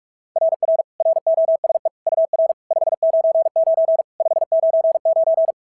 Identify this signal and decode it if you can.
Morse code, heard as a keyed tone, about 47 wpm, 690 Hz. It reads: RR ROSE UR 599 599
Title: SOTA QSO Practice 45wpm